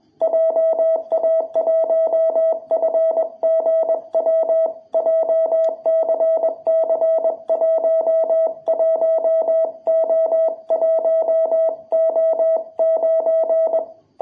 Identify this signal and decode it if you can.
Morse code, heard as a keyed tone, about 21 wpm, 660 Hz. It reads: JA1FGW JCC11O1O9
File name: jh8cbh / 2021cw